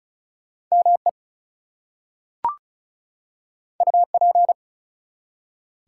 Morse code, heard as a keyed tone, about 35 wpm, 700 Hz. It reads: ME UP
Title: Two Letter Words - 35wpm